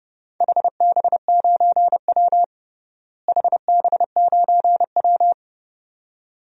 Morse code, heard as a keyed tone, about 30 wpm, 700 Hz. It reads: HB9W HB9W
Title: DXCC-100 Call Signs - 4 Characters - Repeated 2x - 30wpm